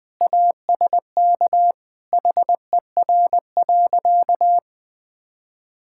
Morse code, heard as a keyed tone, about 21 wpm, 700 Hz. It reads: ASK HER.
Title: Sentences with 3-Word Phrases - Spoken Beginning - 20wpm